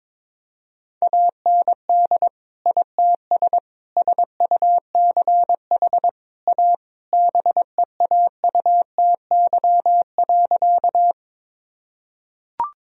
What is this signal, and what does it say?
Morse code, heard as a keyed tone, about 22 wpm, 700 Hz. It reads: AND ITS SUCH A BEAUTY.
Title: Sentences from Top 1000 Words - Review Part 2 - 22wpm